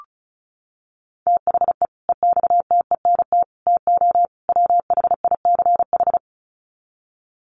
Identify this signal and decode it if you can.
Morse code, heard as a keyed tone, about 34 wpm, 700 Hz. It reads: THE EXTENT TO WHICH
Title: Sets of 4 Words 35wpm